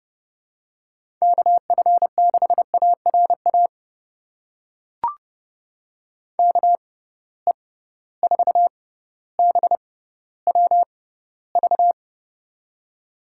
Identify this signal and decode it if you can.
Morse code, heard as a keyed tone, about 30 wpm, spaced as for 15 wpm, 700 Hz. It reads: KF6ARA KE4BWV
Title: USA Call Signs - 6 Characters - 6x Character Spacing - 30wpm